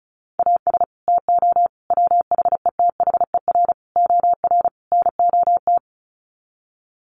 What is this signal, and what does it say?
Morse code, heard as a keyed tone, about 34 wpm, 700 Hz. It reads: AS TO WHETHER OR NOT
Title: Sets of 5 Words Encore 35wpm